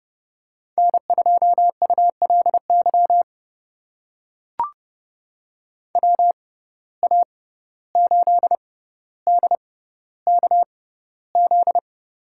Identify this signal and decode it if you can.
Morse code, heard as a keyed tone, about 30 wpm, spaced as for 15 wpm, 700 Hz. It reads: N2ULY WA8DKZ
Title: USA Call Signs - 6x Character Spacing - 30wpm